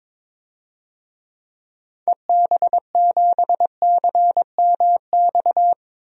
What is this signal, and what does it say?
Morse code, heard as a keyed tone, about 22 wpm, 700 Hz. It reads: EB7CMX
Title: DXCC-100 Call Signs - 6 Characters - 22wpm